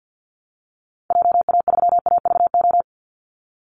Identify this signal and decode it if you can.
Morse code, heard as a keyed tone, about 50 wpm, 700 Hz. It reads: JA3AVO
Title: Call Signs Repeated 3x - 50wpm